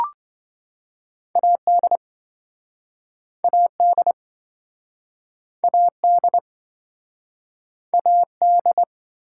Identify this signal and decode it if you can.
Morse code, heard as a keyed tone, about 27 wpm, 700 Hz. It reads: AD AD AD AD